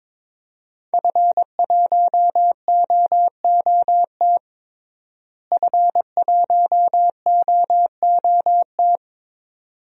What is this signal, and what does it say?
Morse code, heard as a keyed tone, about 22 wpm, 700 Hz. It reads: F1OOT F1OOT
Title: DXCC-100 Call Signs - 5 Characters - Repeated 2x - 22wpm